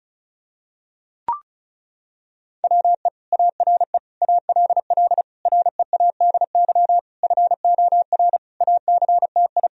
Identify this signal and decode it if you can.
Morse code, heard as a keyed tone, about 34 wpm, 700 Hz. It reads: WE ARE ALL READY FOR ACTI
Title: Sentences from Top 1000 Words - Review Part 4 - 35wpm